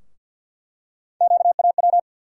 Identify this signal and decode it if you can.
Morse code, heard as a keyed tone, about 52 wpm, 700 Hz. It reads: QAW
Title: USA Call Sign Postfixes - 3 Letters 50wpm